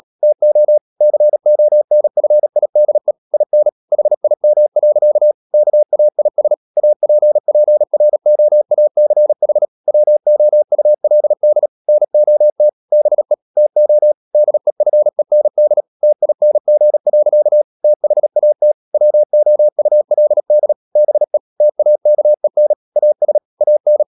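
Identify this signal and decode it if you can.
Morse code, heard as a keyed tone, about 37 wpm, 600 Hz. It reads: TO CONFIDE IN HIM. KAIS APPROACH WOULD NOT BE TO DEFEND TING. THAT WOULD BE TAKEN AS AN
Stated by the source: Morse code practice at 35–39 wpm sending English text of Ken Follett's novel Never, from CW med Gnister.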